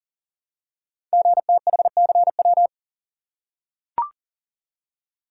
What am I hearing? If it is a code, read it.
Morse code, heard as a keyed tone, about 42 wpm, 700 Hz. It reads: GTHCW
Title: Code Groups - 5 Letters - 40wpm